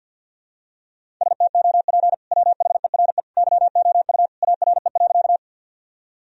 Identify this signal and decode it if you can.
Morse code, heard as a keyed tone, about 52 wpm, 700 Hz. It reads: STOP WHERE YOU ARE.
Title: Sentences from Top 1000 Words - Review Part 1 - 50wpm